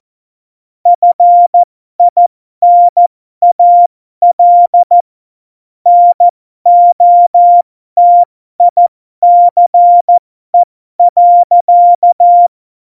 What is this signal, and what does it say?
Morse code, heard as a keyed tone, about 14 wpm, spaced as for 12 wpm, 700 Hz. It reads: FINAL NOTICE.
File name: Morsekurs / Witze